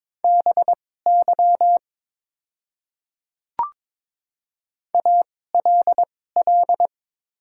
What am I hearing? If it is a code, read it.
Morse code, heard as a keyed tone, about 22 wpm, spaced as for 16 wpm, 700 Hz. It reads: BY ALL
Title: 2x Character Spacing - Top 100 Words - 22wpm